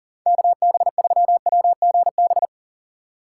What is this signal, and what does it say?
Morse code, heard as a keyed tone, about 40 wpm, 700 Hz. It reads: KB3WGB
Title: USA Call Signs - 3x Character Spacing - 40wpm